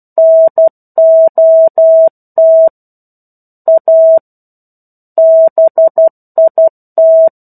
Morse code, heard as a keyed tone, about 12 wpm, 650 Hz. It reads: NOT A BIT
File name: 01-Chapter-01 12 WPM